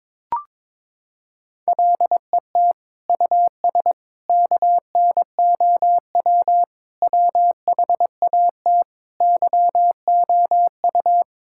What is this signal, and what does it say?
Morse code, heard as a keyed tone, about 22 wpm, 700 Hz. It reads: LET US KNOW WHAT YOU
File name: practice / 22wpm / Sets of 6 Words - 22wpm